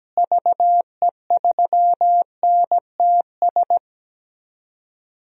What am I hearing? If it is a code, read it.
Morse code, heard as a keyed tone, about 17 wpm, 700 Hz. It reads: VE3NTS
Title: DXCC-100 Call Signs - 6 Characters - Repeated 3x - 17wpm